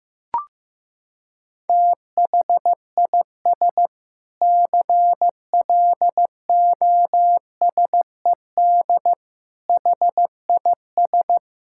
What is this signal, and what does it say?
Morse code, heard as a keyed tone, about 15 wpm, 700 Hz. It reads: THIS CLOSED HIS
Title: Sentences from Top 1600 Words 15wpm